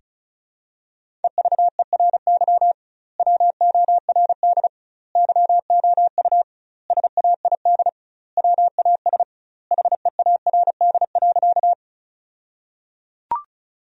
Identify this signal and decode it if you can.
Morse code, heard as a keyed tone, about 36 wpm, 700 Hz. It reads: EVERY WORD YOU SAID WAS HEARD.
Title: 2x Word-Spacing - Sentences from Top 400 Words - 35wpm